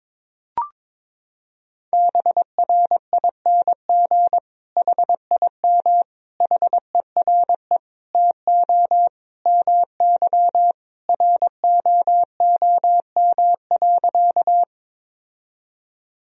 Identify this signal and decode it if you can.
Morse code, heard as a keyed tone, about 21 wpm, 700 Hz. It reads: BRING HIM HERE TO MY ROOM.
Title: Sentences from Top 1000 Words - Review Part 3 - 22wpm